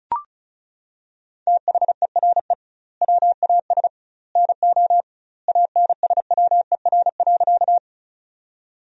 Morse code, heard as a keyed tone, about 34 wpm, 700 Hz. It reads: THERE WAS NO ANSWER.
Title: 2x Word-Spacing - Sentences from Top 300 Words - 35wpm